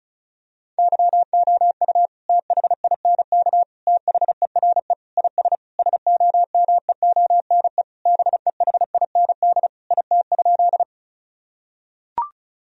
Morse code, heard as a keyed tone, about 36 wpm, 700 Hz. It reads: YOU THINK THERE IS SOMEONE BEHIND IT?
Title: Sentences from Top 1000 Words - Review Part 4 - 35wpm